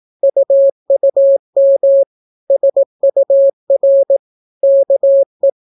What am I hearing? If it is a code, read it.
Morse code, heard as a keyed tone, about 18 wpm, 550 Hz. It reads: UUM SUR KE